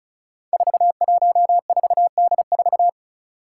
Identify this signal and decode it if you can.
Morse code, heard as a keyed tone, about 36 wpm, 700 Hz. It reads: 414D4
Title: Five Challenging Letter-Number Combinations 35wpm